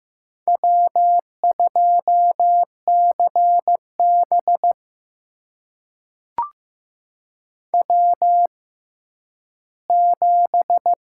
Morse code, heard as a keyed tone, about 15 wpm, spaced as for 10 wpm, 700 Hz. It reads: W2CB W7
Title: USA Call Signs - 6x Character Spacing - 15wpm